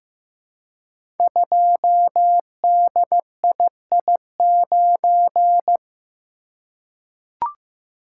Morse code, heard as a keyed tone, about 15 wpm, 700 Hz. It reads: 2DII9
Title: Code Groups - 5 Letter:Numbers - 15wpm